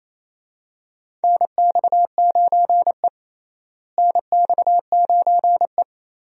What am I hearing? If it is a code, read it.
Morse code, heard as a keyed tone, about 28 wpm, 700 Hz. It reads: NX9E NX9E
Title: USA Call Signs - 4 Characters - Repeated 3x - 28wpm